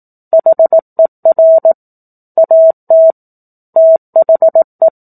Morse code, heard as a keyed tone, about 18 wpm, 650 Hz. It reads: HER AT THE
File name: 04-Chapter-04 18 WPM